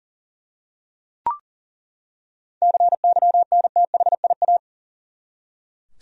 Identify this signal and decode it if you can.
Morse code, heard as a keyed tone, about 40 wpm, 700 Hz. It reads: CYNTHIA
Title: USA Common Names - 40wpm